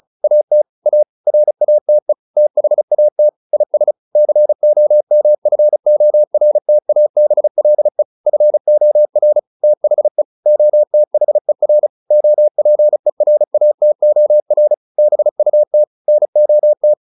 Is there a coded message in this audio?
Yes: Morse code from 0.2 to 16.9 s, heard as a keyed tone, about 34 wpm, 600 Hz. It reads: AT A RATE THAT IS COMFORTABLE FOR THE OTHER OPERATOR BUT NOT